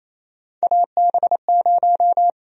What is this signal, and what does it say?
Morse code, heard as a keyed tone, about 28 wpm, 700 Hz. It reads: AB0
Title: USA Call Sign Prefixes - Repeated 2x - 28wpm